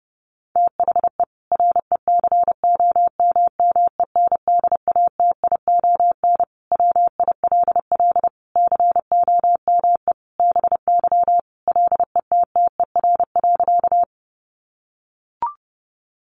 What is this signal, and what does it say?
Morse code, heard as a keyed tone, about 30 wpm, 700 Hz. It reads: THE RECOMMENDATION WILL COME BY LETTER.
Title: Sentences from Top 1400 Words 30wpm